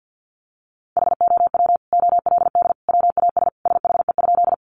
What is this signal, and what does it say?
Morse code, heard as a keyed tone, about 50 wpm, 700 Hz. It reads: HOW OLD WAS SHE?
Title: Sentences from Top 200 Words 50wpm